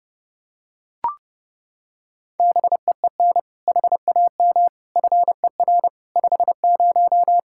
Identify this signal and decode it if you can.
Morse code, heard as a keyed tone, about 32 wpm, 700 Hz. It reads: BEEN HAM FER 50
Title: K1USN SST QSO Practice 30wpm